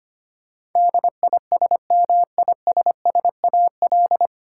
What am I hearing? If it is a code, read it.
Morse code, heard as a keyed tone, about 25 wpm, 700 Hz. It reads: DISMISSAL
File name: practice / 25wpm / 9 Letter Words - First 3 Letters Given - 25wpm